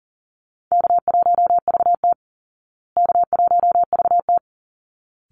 Morse code, heard as a keyed tone, about 40 wpm, 700 Hz. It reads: K1VT K1VT
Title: Easy Call Signs Repeated 2x - 40wpm